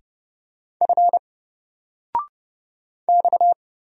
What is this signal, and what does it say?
Morse code, heard as a keyed tone, about 30 wpm, 700 Hz. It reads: F X
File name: Single-Letter-Number-30wpm